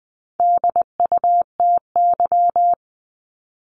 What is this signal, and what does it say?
Morse code, heard as a keyed tone, about 20 wpm, 700 Hz. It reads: DUTY
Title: 260 Lesson 42e - Words 20-20wpm